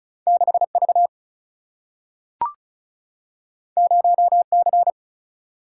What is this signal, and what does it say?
Morse code, heard as a keyed tone, about 36 wpm, 700 Hz. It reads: 6 V 0 C
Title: Single-Letter Single-Number 35wpm